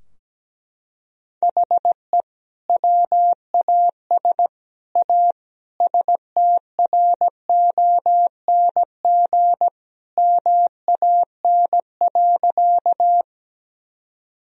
Morse code, heard as a keyed tone, about 17 wpm, 700 Hz. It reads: HE WAS A STRONG MAN.
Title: Sentences from Top 1000 Words - Review Part 3 - 17wpm